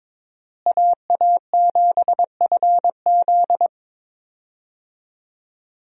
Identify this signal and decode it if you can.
Morse code, heard as a keyed tone, about 22 wpm, 700 Hz. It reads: AA7FZ
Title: USA Call Signs - 5 Characters - Repeated 2x - 22wpm